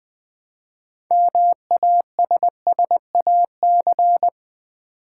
Morse code, heard as a keyed tone, about 20 wpm, 700 Hz. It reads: MASSAC
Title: Build A Word - 2 Letters At A Time - 8-Letter Words - 20wpm